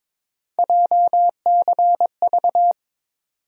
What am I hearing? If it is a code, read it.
Morse code, heard as a keyed tone, about 22 wpm, 700 Hz. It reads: JCV
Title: USA Call Sign Postfixes - 3 Letters 22wpm